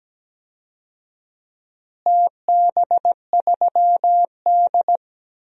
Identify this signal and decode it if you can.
Morse code, heard as a keyed tone, about 17 wpm, 700 Hz. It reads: TB3D
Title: DXCC-100 Call Signs - 4 Characters - Repeated 2x - 17wpm